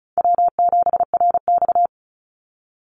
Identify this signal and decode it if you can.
Morse code, heard as a keyed tone, about 34 wpm, 700 Hz. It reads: W7RX
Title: Call Signs Repeated 2x - 35wpm